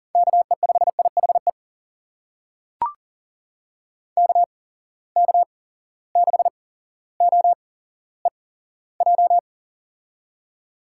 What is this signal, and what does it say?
Morse code, heard as a keyed tone, about 40 wpm, spaced as for 17 wpm, 700 Hz. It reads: KE5IHE KK6OEJ
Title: USA Call Signs - 6 Characters - 8x Character Spacing - 40wpm